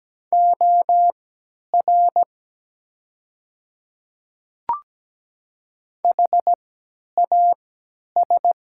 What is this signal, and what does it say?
Morse code, heard as a keyed tone, about 17 wpm, spaced as for 10 wpm, 700 Hz. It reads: OR HAS
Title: 3x Character Spacing - Top 200 Words - 17wpm